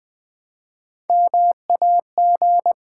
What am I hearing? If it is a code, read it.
Morse code, heard as a keyed tone, about 20 wpm, 700 Hz. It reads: MAG
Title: Build A Word - 3 Letters At A Time - 8-Letter Words - 20wpm